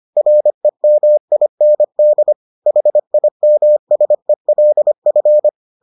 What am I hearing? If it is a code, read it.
Morse code, heard as a keyed tone, about 25 wpm, 600 Hz. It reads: REMIND HIMSELF